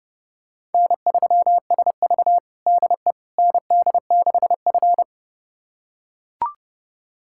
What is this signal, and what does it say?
Morse code, heard as a keyed tone, about 30 wpm, 700 Hz. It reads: N3SV DE ND6F
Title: USA Callsign DE Callsign - 4 Characters - 30wpm